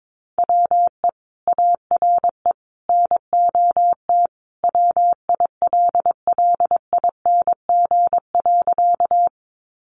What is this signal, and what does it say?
Morse code, heard as a keyed tone, about 22 wpm, 700 Hz. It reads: WE ARE NOT WILLING.
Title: Sentences from Top 1000 Words 22wpm